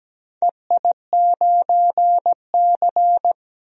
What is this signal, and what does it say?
Morse code, heard as a keyed tone, about 17 wpm, 700 Hz. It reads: EI9C
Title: DXCC-100 Call Signs - 4 Characters - 17wpm